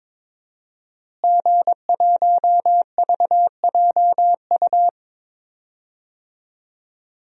Morse code, heard as a keyed tone, about 22 wpm, 700 Hz. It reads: G1VJU
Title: DXCC-100 Call Signs - 5 Characters - Repeated 3x - 22wpm